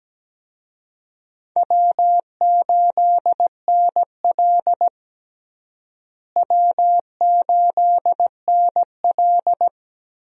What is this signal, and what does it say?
Morse code, heard as a keyed tone, about 17 wpm, 700 Hz. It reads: W8NL W8NL
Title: USA Call Signs - 4 Characters - Repeated 3x - 17wpm